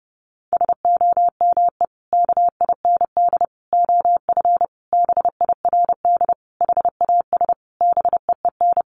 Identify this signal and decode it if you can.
Morse code, heard as a keyed tone, about 28 wpm, 700 Hz. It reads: SOME KIND OF BIRD HAS BEEN
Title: Sentences from Top 1200 Words 30wpm